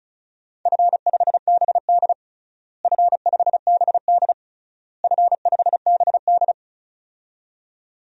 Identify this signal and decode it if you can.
Morse code, heard as a keyed tone, about 36 wpm, 700 Hz. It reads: F5BD F5BD F5BD
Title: DXCC-100 Call Signs - 4 Characters - Repeated 3x - 35wpm